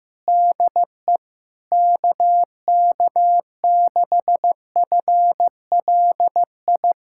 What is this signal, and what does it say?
Morse code, heard as a keyed tone, about 15 wpm, 700 Hz. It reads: DE KK6FLI
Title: USA Callsign DE Callsign - 6 Characters - 15wpm